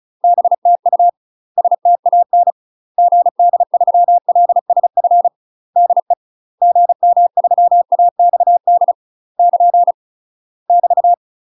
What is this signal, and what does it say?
Morse code, heard as a keyed tone, about 35 wpm, 700 Hz. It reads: BTU STAN GD3LSF DE GM3AXD <KN> <BT>
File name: UK MASTER QSOs TRAINING / 35WPM / UK MASTER QSOs 0017